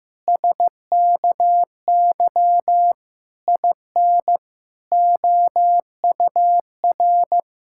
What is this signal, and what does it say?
Morse code, heard as a keyed tone, about 15 wpm, 700 Hz. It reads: SKY IN OUR
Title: Pangram Challenge v2 - 15wpm